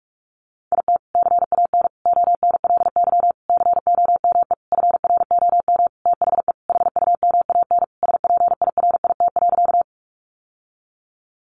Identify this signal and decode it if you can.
Morse code, heard as a keyed tone, about 44 wpm, 700 Hz. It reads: IT CAN ONLY COME FROM THE HUMAN SPIRIT.
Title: Sentences from Top 1400 Words 45wpm